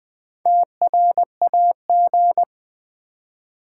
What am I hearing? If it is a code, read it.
Morse code, heard as a keyed tone, about 20 wpm, 700 Hz. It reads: TRAG
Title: Build A Word - 2 Letters At A Time - 6-Letter Words - 20wpm